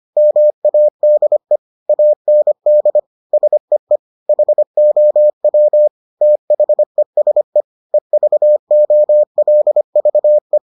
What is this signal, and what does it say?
Morse code, heard as a keyed tone, about 25 wpm, 600 Hz. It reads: MADE AND SEE HOW THESE EVOLVE